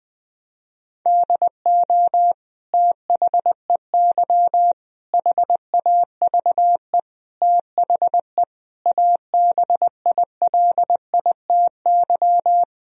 Morse code, heard as a keyed tone, about 20 wpm, 700 Hz. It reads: DO THEY HAVE THE ABILITY
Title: Sentences from Top 1000 Words - Review Part 3 - 20wpm